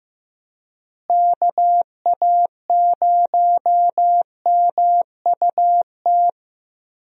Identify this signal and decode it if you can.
Morse code, heard as a keyed tone, about 16 wpm, 700 Hz. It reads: KA0MUT
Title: USA Call Signs - 6 Characters - Repeated 3x - 15wpm